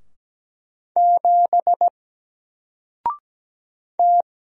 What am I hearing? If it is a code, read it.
Morse code, heard as a keyed tone, about 17 wpm, 700 Hz. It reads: 7 T
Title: Single Letter-Number Alt 17wpm